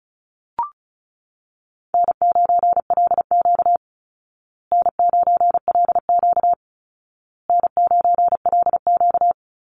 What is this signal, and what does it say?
Morse code, heard as a keyed tone, about 34 wpm, 700 Hz. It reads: N9LQ N9LQ N9LQ
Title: Call Signs Repeated 2x - 35wpm